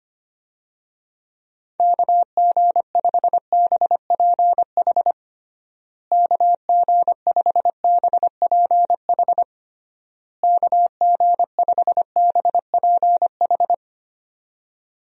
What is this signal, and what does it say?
Morse code, heard as a keyed tone, about 25 wpm, 700 Hz. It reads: KG5BPH KG5BPH KG5BPH
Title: USA Call Signs - 6 Characters - Repeated 3x - 25wpm